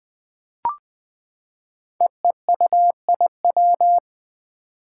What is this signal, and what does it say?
Morse code, heard as a keyed tone, about 20 wpm, 700 Hz. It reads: EEUIW
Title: Five Challenging Letter-Number Combinations 20wpm